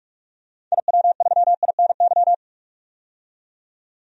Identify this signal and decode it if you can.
Morse code, heard as a keyed tone, about 45 wpm, 700 Hz. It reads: IW3INY
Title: DXCC-100 Call Signs - 6 Characters - 45wpm